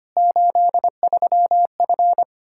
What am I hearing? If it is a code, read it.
Morse code, heard as a keyed tone, about 25 wpm, 700 Hz. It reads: 83F
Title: K1USN SST QSO Practice 25wpm